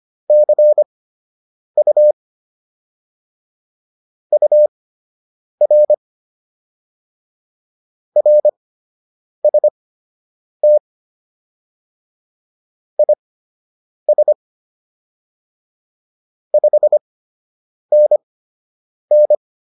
Morse code, heard as a keyed tone, about 25 wpm, spaced as for 8 wpm, 600 Hz. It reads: CU UR RST IS 5NN